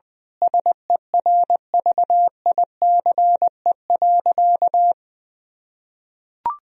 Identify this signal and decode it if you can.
Morse code, heard as a keyed tone, about 20 wpm, 700 Hz. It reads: SERVICE.
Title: Sentences from Top 1000 Words - Review Part 4 - 20wpm